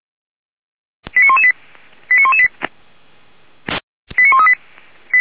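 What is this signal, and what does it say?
Five-tone selective call (ZVEI), heard as quick runs of five tones, tones of about 70 ms. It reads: 89118 89118 89138
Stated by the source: ZVEI five-tone call received by SDR.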